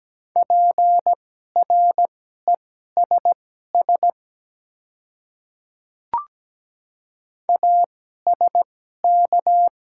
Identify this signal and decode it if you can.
Morse code, heard as a keyed tone, about 17 wpm, spaced as for 12 wpm, 700 Hz. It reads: PRESS ASK
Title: 2x Character Spacing - Top 300 Words - 17wpm